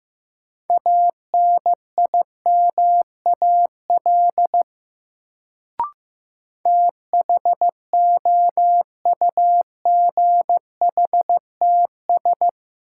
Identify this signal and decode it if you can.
Morse code, heard as a keyed tone, about 15 wpm, 700 Hz. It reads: ANIMAL THOUGHTS
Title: Sentences from Top 1000 Words - Review Part 1 - New Word Drill - 15wpm